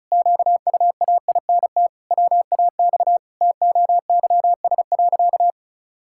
Morse code, heard as a keyed tone, about 34 wpm, 700 Hz. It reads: QUAINT WAX TOYS.
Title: Pangram Challenge v2 - 35wpm